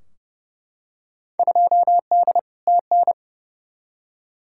Morse code, heard as a keyed tone, about 29 wpm, 700 Hz. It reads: 2D TN
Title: Field Day Exchanges Encore - 30wpm